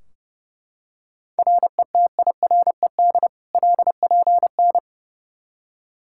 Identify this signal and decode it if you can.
Morse code, heard as a keyed tone, about 30 wpm, 700 Hz. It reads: RETIRED LPN
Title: Practice QSO - Exchange 3 - More Info - 30wpm